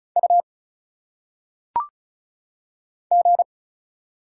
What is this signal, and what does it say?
Morse code, heard as a keyed tone, about 36 wpm, 700 Hz. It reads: U G
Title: Single Letters - Phonetic Pronunciation 35wpm